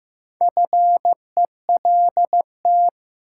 Morse code, heard as a keyed tone, about 15 wpm, 700 Hz. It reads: FELT
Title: Sets of 5 Words Encore - New Word Drill - 15wpm